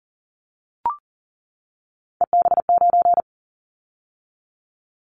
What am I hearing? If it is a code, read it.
Morse code, heard as a keyed tone, about 40 wpm, 700 Hz. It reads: EB9
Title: Callsign Prefixes for Top 100 DXCC - 40wpm